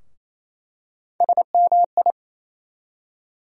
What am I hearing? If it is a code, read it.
Morse code, heard as a keyed tone, about 28 wpm, 700 Hz. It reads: SMI
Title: USA Call Sign Postfixes - 3 Letters 28wpm